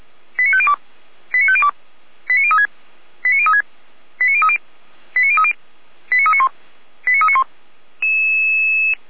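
Five-tone selective call (ZVEI), heard as quick runs of five tones, tones of about 70 ms. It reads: 89502 89502 89036 89036 89030 89030 89391 89391
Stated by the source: ZVEI five-tone selective call received on SDR.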